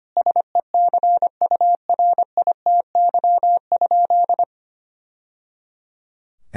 Morse code, heard as a keyed tone, about 25 wpm, 700 Hz. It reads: SECURITY?
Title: Sentences from Top 1000 Words - Review Part 4 - 25wpm